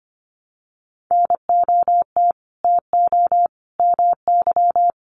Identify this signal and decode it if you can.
Morse code, heard as a keyed tone, about 25 wpm, 700 Hz. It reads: NOT TO MY
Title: Sentences from Top 600 Words 25wpm